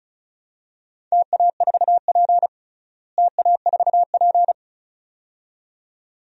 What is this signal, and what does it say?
Morse code, heard as a keyed tone, about 36 wpm, 700 Hz. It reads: TA4P TA4P
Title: DXCC-100 Call Signs - 4 Characters - Repeated 2x - 35wpm